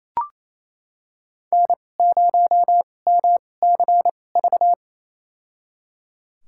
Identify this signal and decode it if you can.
Morse code, heard as a keyed tone, about 28 wpm, spaced as for 20 wpm, 700 Hz. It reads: N0MCV
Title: USA Call Signs - 5 Characters - 2x Character Spacing - 28wpm